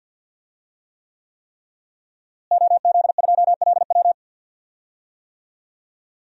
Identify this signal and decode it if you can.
Morse code, heard as a keyed tone, about 52 wpm, 700 Hz. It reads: OZ2LW
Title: DXCC-100 Call Signs - 5 Characters - Repeated 3x - 50wpm